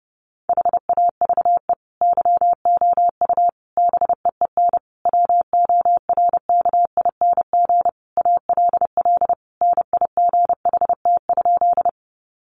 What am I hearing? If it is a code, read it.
Morse code, heard as a keyed tone, about 28 wpm, 700 Hz. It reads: HAVE YOU BEEN WORKING ALL NIGHT?
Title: Sentences from Top 700 Words 30wpm